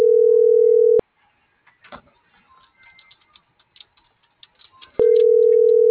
Ringback tone (North America), 440 + 480 Hz.